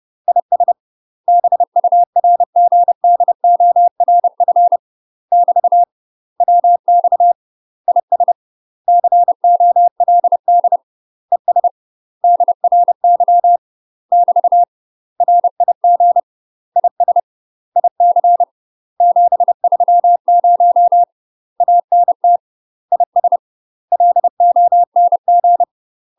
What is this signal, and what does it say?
Morse code, heard as a keyed tone, about 30 wpm, 700 Hz. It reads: IS BURGDORF <BT> WX IS COLD ES DRY <BT> RIG IS IC 730 ANT IS LONG
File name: UK MASTER QSOs TRAINING / 30WPM / UK MASTER QSOs 0049